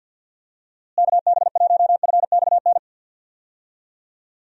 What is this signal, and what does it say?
Morse code, heard as a keyed tone, about 51 wpm, 700 Hz. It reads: KB1FXN
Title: USA Call Signs - 6 Characters - Repeated 3x - 50wpm